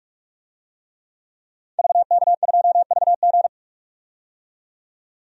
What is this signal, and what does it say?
Morse code, heard as a keyed tone, about 45 wpm, 700 Hz. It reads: VK2VG
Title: DXCC-100 Call Signs - 5 Characters - 45wpm